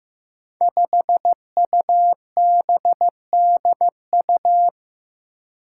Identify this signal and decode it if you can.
Morse code, heard as a keyed tone, about 15 wpm, 700 Hz. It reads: 5UBDU
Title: Five Challenging Letter-Number Combinations 15wpm